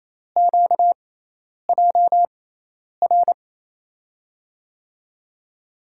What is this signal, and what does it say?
Morse code, heard as a keyed tone, about 28 wpm, spaced as for 10 wpm, 700 Hz. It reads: QJR
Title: USA Call Signs - 5 Characters - 6x Character Spacing - 28wpm